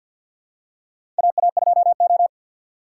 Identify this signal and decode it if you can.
Morse code, heard as a keyed tone, about 47 wpm, 700 Hz. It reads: AA2O
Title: USA Call Signs - 4 Characters - 2x Character Spacing - 50wpm